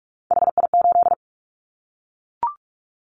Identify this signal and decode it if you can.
Morse code, heard as a keyed tone, about 44 wpm, 700 Hz. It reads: HI8
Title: Callsign Prefixes for Top 100 DXCC - 45wpm